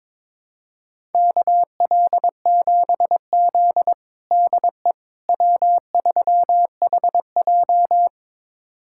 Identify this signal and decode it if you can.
Morse code, heard as a keyed tone, about 22 wpm, 700 Hz. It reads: KL7Z DE W3HJ
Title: USA Callsign DE Callsign - 4 Characters - 22wpm